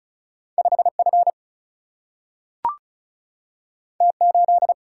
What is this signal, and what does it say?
Morse code, heard as a keyed tone, about 36 wpm, 700 Hz. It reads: 5F T8
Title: Single-Letter Single-Number 35wpm